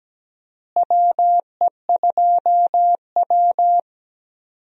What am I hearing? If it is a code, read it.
Morse code, heard as a keyed tone, about 17 wpm, 700 Hz. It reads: WE2W
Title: USA Call Signs - 4 Characters - Repeated 2x - 17wpm